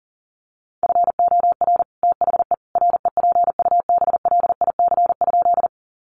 Morse code, heard as a keyed tone, about 40 wpm, 700 Hz. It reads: FOR THE REPUBLIC?
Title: Sentences from Top 1500 Words 40wpm